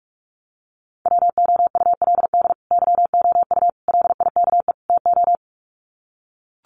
Morse code, heard as a keyed tone, about 44 wpm, 700 Hz. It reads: WOULD YOU LIKE TO
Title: Sets of 4 Words 45wpm